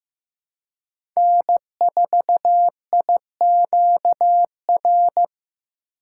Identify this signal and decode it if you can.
Morse code, heard as a keyed tone, about 15 wpm, 700 Hz. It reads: N4IQR
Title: USA Call Signs - 5 Characters - 2x Character Spacing - 15wpm